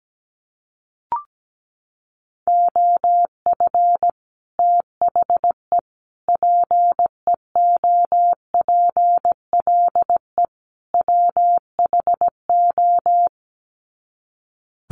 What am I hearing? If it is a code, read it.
Morse code, heard as a keyed tone, about 17 wpm, 700 Hz. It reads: OF THE PEOPLE WHO
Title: Sets of 4 Words 17wpm